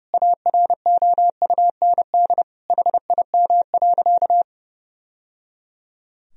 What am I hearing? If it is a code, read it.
Morse code, heard as a keyed tone, about 30 wpm, 700 Hz. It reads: AROUND HIM.
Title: Sentences from Top 1000 Words - Review Part 3 - 30wpm